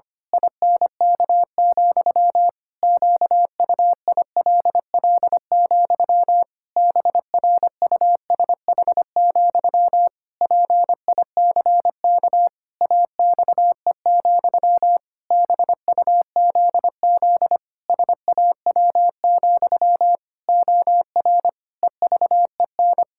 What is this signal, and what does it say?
Morse code, heard as a keyed tone, about 25 wpm, 700 Hz. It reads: INK, QUILL, BRUSH, PICK AXE, BUZZ SAW, OR EVEN
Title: Pangram Challenge v2 - 25wpm